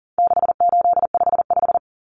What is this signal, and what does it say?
Morse code, heard as a keyed tone, about 40 wpm, 700 Hz. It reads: 6855
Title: Common Cut Numbers - 4-Digit 40wpm